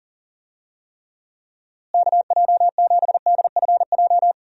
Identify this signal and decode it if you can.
Morse code, heard as a keyed tone, about 40 wpm, 700 Hz. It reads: KJ7DFJ
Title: USA Call Signs - 6 Characters - Repeated 2x - 40wpm